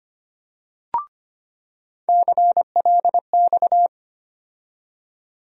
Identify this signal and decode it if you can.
Morse code, heard as a keyed tone, about 25 wpm, 700 Hz. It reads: CLX
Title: USA Call Sign Postfixes - 3 Letters 25wpm